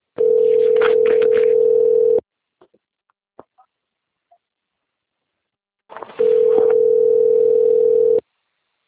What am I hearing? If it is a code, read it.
Ringback tone (North America), 440 + 480 Hz.
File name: rg-1600-unknown- - - .29370